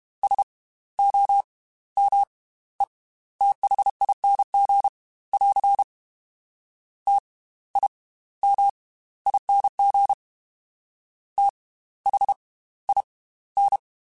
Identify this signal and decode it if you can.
Morse code, heard as a keyed tone, about 32 wpm, spaced as for 16 wpm, 800 Hz. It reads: SOMETHING <AR> TIMING THIN